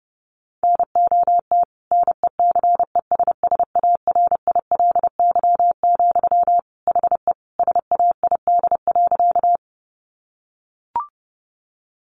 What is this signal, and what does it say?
Morse code, heard as a keyed tone, about 30 wpm, 700 Hz. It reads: NOT NECESSARILY, HE SAID.
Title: Sentences from Top 1400 Words 30wpm